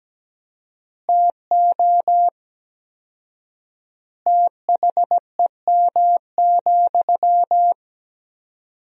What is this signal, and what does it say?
Morse code, heard as a keyed tone, about 17 wpm, 700 Hz. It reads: TO THEM,
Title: 4x Word-Spacing - Sentences from Top 400 Words - 17wpm